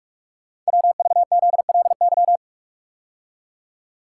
Morse code, heard as a keyed tone, about 44 wpm, 690 Hz. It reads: WVZLY
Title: Code Groups - 5 Letters - 45wpm